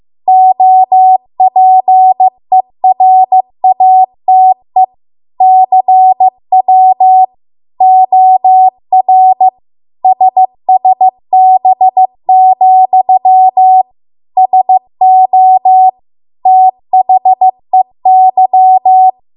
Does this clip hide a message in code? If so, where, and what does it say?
Yes: Morse code from 0.3 to 19.1 s, heard as a keyed tone, about 15 wpm, 750 Hz. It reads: OPERATE CW OR SSB, SO THEY